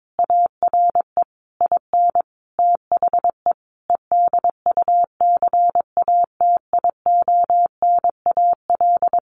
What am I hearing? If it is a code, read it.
Morse code, heard as a keyed tone, about 22 wpm, 700 Hz. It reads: ARE IN THE EDUCATIONAL
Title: Sentences from Top 1200 Words 22wpm